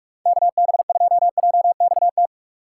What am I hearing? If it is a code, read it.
Morse code, heard as a keyed tone, about 45 wpm, 700 Hz. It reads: KB2JXT
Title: USA Call Signs - 6 Characters - 3x Character Spacing - 45wpm